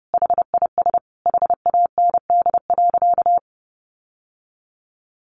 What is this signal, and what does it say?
Morse code, heard as a keyed tone, about 28 wpm, 700 Hz. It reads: HIS HAND.
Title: Sentences from Top 1400 Words 30wpm